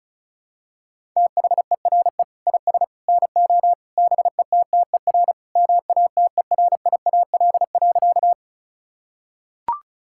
Morse code, heard as a keyed tone, about 36 wpm, 700 Hz. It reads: THERE IS NO BETTER MATERIAL.
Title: Sentences from Top 1000 Words - Review Part 5 - 35wpm